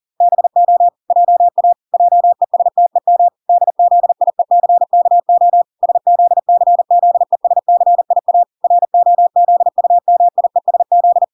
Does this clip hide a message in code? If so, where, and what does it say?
Yes: Morse code from 0.2 to 11.2 s, heard as a keyed tone, about 40 wpm, 700 Hz. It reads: BO JA JESTEM DZIECKO SZCZESCIA ROZUMIESZ